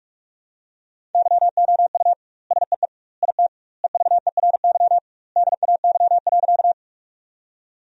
Morse code, heard as a keyed tone, about 44 wpm, 700 Hz. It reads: YOU SEE IT EVERY DAY.
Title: 2x Word-Spacing - Sentences from Top 200 Words - 45wpm